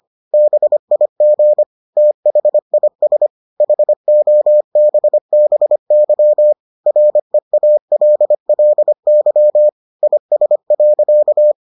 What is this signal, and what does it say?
Morse code, heard as a keyed tone, about 25 wpm, 600 Hz. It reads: BIG THIS HOBBY REALLY IS.